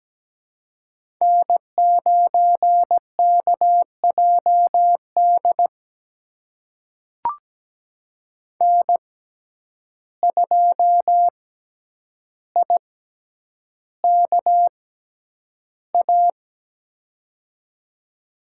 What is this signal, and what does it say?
Morse code, heard as a keyed tone, about 17 wpm, spaced as for 9 wpm, 700 Hz. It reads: N9KJD N2IKA
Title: USA Call Signs - 6x Character Spacing - 17wpm